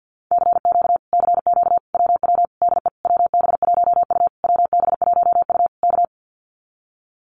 Morse code, heard as a keyed tone, about 50 wpm, 700 Hz. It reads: CQ CQ WW DE WB1U WB1U K
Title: Common QSO Phrases Encore 50wpm